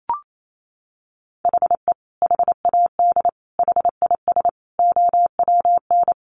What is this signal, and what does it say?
Morse code, heard as a keyed tone, about 28 wpm, 700 Hz. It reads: HE HAD HIS OWN
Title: Sentences from Top 1200 Words 28wpm